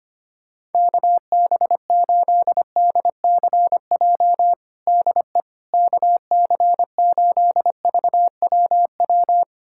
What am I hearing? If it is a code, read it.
Morse code, heard as a keyed tone, about 25 wpm, 700 Hz. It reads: KB8DCJ DE KC8VWW
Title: USA Callsign DE Callsign - 6 Characters - 25wpm